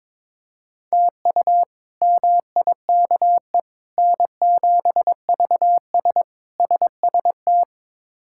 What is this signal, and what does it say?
Morse code, heard as a keyed tone, about 22 wpm, 700 Hz. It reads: TU MIKE N7VS SST
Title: K1USN SST QSO Practice - 22wpm